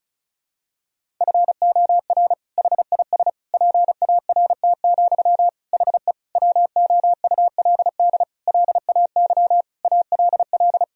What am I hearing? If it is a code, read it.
Morse code, heard as a keyed tone, about 34 wpm, 700 Hz. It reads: FOR HIS PART, HE WOULD LAY ALL
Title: Sentences from Top 1000 Words - Review Part 2 - 35wpm